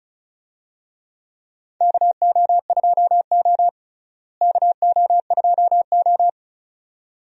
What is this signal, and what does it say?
Morse code, heard as a keyed tone, about 36 wpm, 700 Hz. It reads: KO2O KO2O
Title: USA Call Signs - 4 Characters - Repeated 2x - 35wpm